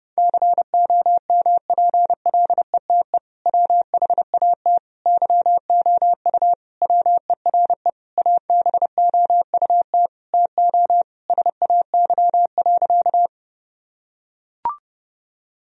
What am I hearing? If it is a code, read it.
Morse code, heard as a keyed tone, about 30 wpm, 700 Hz. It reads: COMPLETE WHAT YOU WERE ABOUT TO SAY.
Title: Sentences from Top 1000 Words - Review Part 2 - 30wpm